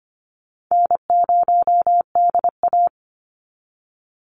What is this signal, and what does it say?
Morse code, heard as a keyed tone, about 25 wpm, 700 Hz. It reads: N0DA
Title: NA SOTA Chaser Callsigns 25wpm